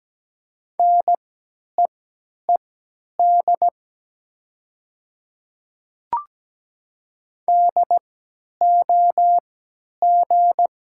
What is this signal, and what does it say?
Morse code, heard as a keyed tone, about 17 wpm, spaced as for 10 wpm, 700 Hz. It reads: NEED DOG
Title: 3x Character Spacing - Top 500 Words - 17wpm